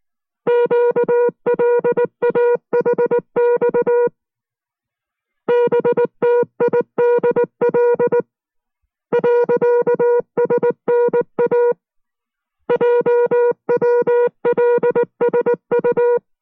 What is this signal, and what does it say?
Morse code, heard as a keyed tone, about 19 wpm, 480 Hz. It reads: QLAHX BTIDL .SNA JWLSU